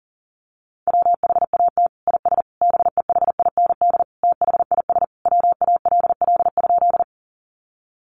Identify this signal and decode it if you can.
Morse code, heard as a keyed tone, about 40 wpm, 700 Hz. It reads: WHAT IS BEHIND THIS WALL?
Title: Sentences from Top 1100 Words 40wpm